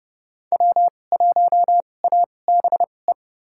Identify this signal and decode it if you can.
Morse code, heard as a keyed tone, about 30 wpm, spaced as for 22 wpm, 700 Hz. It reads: W1ABE
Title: USA Call Signs - 2x Character Spacing - 30wpm